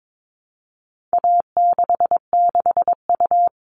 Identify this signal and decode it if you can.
Morse code, heard as a keyed tone, about 22 wpm, 700 Hz. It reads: A66U
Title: All Cut Numbers - 4-Digit 22wpm